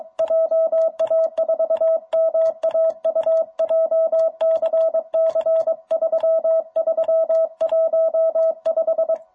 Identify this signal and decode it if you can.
Morse code, heard as a keyed tone, about 22 wpm, 660 Hz. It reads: JA4MAU JCC 3315
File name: jh8cbh / 2021cw